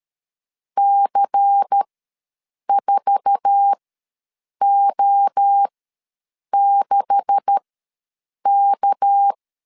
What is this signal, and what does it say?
Morse code, heard as a keyed tone, about 13 wpm, spaced as for 7 wpm, 800 Hz. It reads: C4O6K
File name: LEZIONI / 9 286 ws 12wpm 3351